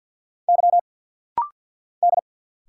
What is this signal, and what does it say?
Morse code, heard as a keyed tone, about 48 wpm, 690 Hz. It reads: Y D
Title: Single Letters - Rapid-Fire 50wpm